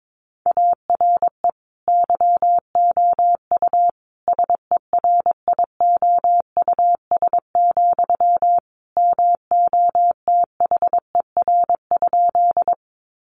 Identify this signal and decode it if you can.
Morse code, heard as a keyed tone, about 22 wpm, 700 Hz. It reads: ARE YOU SERIOUS, MOTHER?
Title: Sentences from Top 1100 Words 22wpm